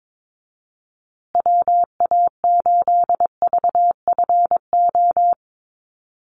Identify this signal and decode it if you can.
Morse code, heard as a keyed tone, about 22 wpm, 700 Hz. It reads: WA8VFO
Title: USA Call Signs Repeated 3x 22wpm